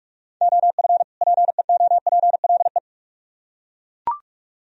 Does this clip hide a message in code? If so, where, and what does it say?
Yes: Morse code from 0.4 to 2.8 s, heard as a keyed tone, about 41 wpm, 700 Hz. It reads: OF PEOPLE